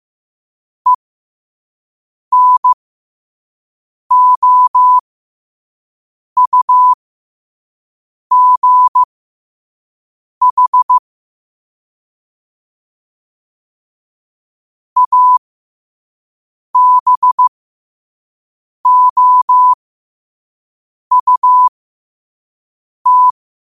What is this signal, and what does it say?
Morse code, heard as a keyed tone, about 15 wpm, spaced as for 5 wpm, 1000 Hz. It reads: ENOUGH ABOUT
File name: 5wpm